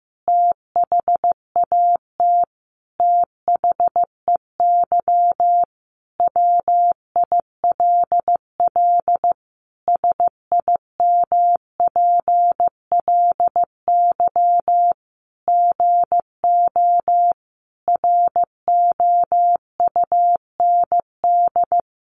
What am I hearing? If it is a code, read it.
Morse code, heard as a keyed tone, about 15 wpm, 700 Hz. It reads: THAT THEY WILL SIMPLY GO ROUND
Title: Sentences from Top 600 Words 15wpm